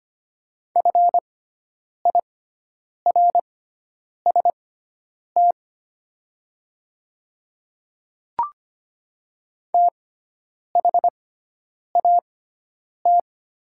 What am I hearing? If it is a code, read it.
Morse code, heard as a keyed tone, about 25 wpm, spaced as for 9 wpm, 700 Hz. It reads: FIRST THAT
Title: 6x Character Spacing - Top 200 Words - 25wpm